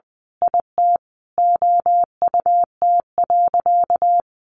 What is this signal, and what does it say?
Morse code, heard as a keyed tone, about 20 wpm, 700 Hz. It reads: IT OUT.
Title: Sentences from Top 1400 Words 20wpm